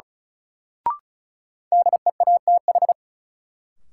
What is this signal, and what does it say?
Morse code, heard as a keyed tone, about 34 wpm, 700 Hz. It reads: DEATH
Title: Sentences from Top 1100 Words - New Word Drill Alt - 35wpm